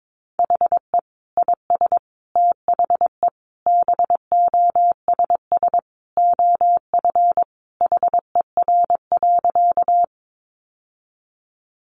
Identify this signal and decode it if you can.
Morse code, heard as a keyed tone, about 22 wpm, 700 Hz. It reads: HE IS THE BOSS OF HER.
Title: Sentences from Top 700 Words 22wpm